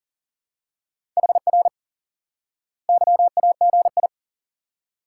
Morse code, heard as a keyed tone, about 40 wpm, 700 Hz. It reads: HR YAGI
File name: IQE Course 3 - Lesson 89 - YAGI - Two Elements - 40wpm